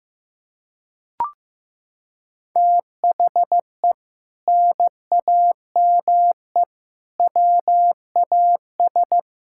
Morse code, heard as a keyed tone, about 15 wpm, 700 Hz. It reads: THE NAME WAS
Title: Sentences from Top 1600 Words 15wpm